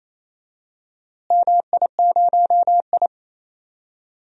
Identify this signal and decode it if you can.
Morse code, heard as a keyed tone, about 28 wpm, 700 Hz. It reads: MI0I
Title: DXCC-100 Call Signs - 4 Characters - Repeated 2x - 28wpm